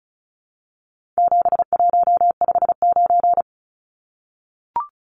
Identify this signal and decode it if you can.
Morse code, heard as a keyed tone, about 34 wpm, 700 Hz. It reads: 7159
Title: Numbers - 4-Digit 35wpm